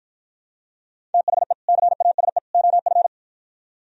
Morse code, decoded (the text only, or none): THE CASE OF